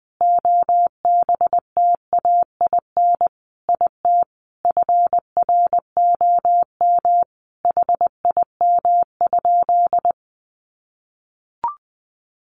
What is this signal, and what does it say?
Morse code, heard as a keyed tone, about 20 wpm, 700 Hz. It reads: OBTAIN IT FROM HIM?
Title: Sentences from Top 1300 Words 20wpm